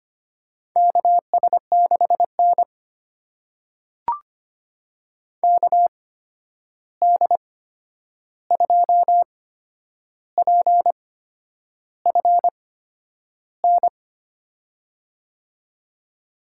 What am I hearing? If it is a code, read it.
Morse code, heard as a keyed tone, about 25 wpm, spaced as for 9 wpm, 700 Hz. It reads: KS6N KD2PFN
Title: USA Call Signs - 8x Character Spacing - 25wpm